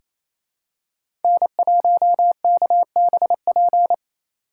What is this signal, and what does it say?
Morse code, heard as a keyed tone, about 28 wpm, 700 Hz. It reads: N1KBP
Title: USA Call Signs - 5 Characters - 2x Character Spacing - 28wpm